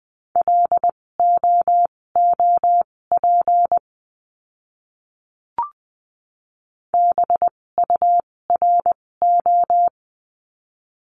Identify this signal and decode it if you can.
Morse code, heard as a keyed tone, about 20 wpm, spaced as for 16 wpm, 700 Hz. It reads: LOOP BURO
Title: 256 Lesson 42a - Words 12-20wpm